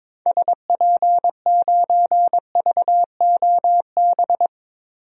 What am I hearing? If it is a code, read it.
Morse code, heard as a keyed tone, about 22 wpm, 700 Hz. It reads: SP9VOB
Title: DXCC-100 Call Signs - 6 Characters - Repeated 2x - 22wpm